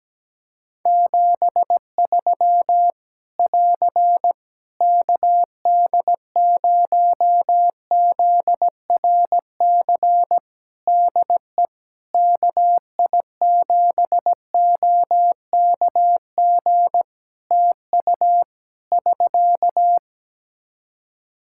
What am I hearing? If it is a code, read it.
Morse code, heard as a keyed tone, about 17 wpm, 700 Hz. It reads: 73 <AR> KD0ZRC DE KI7OKG TU <SK>
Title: K1USN SST QSO Practice 17wpm